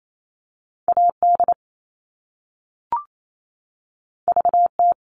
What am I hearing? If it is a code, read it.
Morse code, heard as a keyed tone, about 28 wpm, 700 Hz. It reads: AD VT
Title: All Cut Numbers - 2-Digit 28wpm